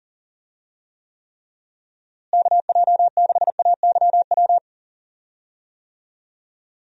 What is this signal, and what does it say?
Morse code, heard as a keyed tone, about 40 wpm, 700 Hz. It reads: KJ6AYW
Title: USA Call Signs - 6 Characters - Repeated 3x - 40wpm